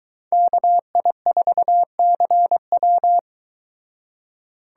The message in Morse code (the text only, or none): KI4CW